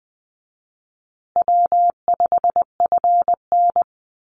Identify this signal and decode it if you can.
Morse code, heard as a keyed tone, about 20 wpm, 700 Hz. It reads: W5FN
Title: Call Signs Repeated 2x - 20wpm